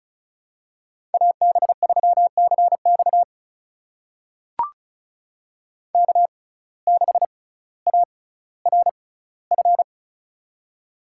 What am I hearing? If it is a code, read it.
Morse code, heard as a keyed tone, about 36 wpm, spaced as for 18 wpm, 700 Hz. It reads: AB3CX K6ARF
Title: USA Call Signs - 5 Characters - 6x Character Spacing - 35wpm